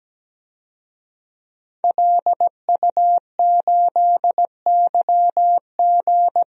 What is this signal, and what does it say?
Morse code, heard as a keyed tone, about 17 wpm, 700 Hz. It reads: LU8YG
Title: DXCC-100 Call Signs - 5 Characters - Repeated 3x - 17wpm